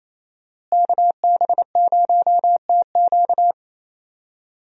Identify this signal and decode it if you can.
Morse code, heard as a keyed tone, about 28 wpm, 700 Hz. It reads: KB0TQ
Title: USA Call Signs - 5 Characters - Repeated 3x - 28wpm